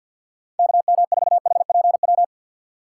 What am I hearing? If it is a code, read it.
Morse code, heard as a keyed tone, about 51 wpm, 700 Hz. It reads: KM4HPW
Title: USA Call Signs - 6 Characters - 4x Character Spacing - 50wpm